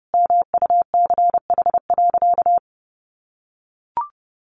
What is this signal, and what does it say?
Morse code, heard as a keyed tone, about 30 wpm, 700 Hz. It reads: MUCH.
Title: Sentences from Top 1200 Words 30wpm